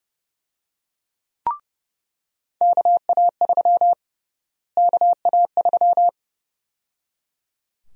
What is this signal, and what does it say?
Morse code, heard as a keyed tone, about 30 wpm, 700 Hz. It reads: KA3 KA3
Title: USA Call Sign Prefixes - Repeated 2x - 30wpm